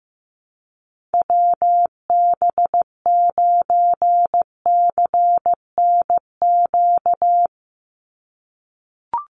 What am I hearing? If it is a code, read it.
Morse code, heard as a keyed tone, about 15 wpm, 700 Hz. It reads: WB9CNQ
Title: USA Call Signs 15wpm